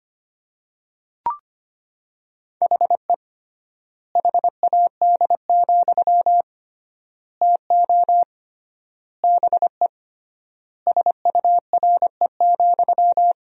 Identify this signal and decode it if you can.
Morse code, heard as a keyed tone, about 25 wpm, 700 Hz. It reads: HE HAD, TO BE SURE,
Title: 3x Word-Spacing - Sentences from Top 500 Words - 25wpm